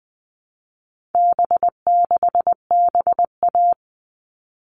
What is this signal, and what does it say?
Morse code, heard as a keyed tone, about 20 wpm, 700 Hz. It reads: B6BA
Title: All Cut Numbers - 4-Digit 20wpm